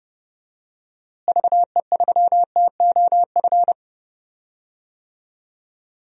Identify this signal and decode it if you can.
Morse code, heard as a keyed tone, about 30 wpm, 700 Hz. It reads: VE3TOF
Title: DXCC-100 Call Signs - 6 Characters - Repeated 3x - 30wpm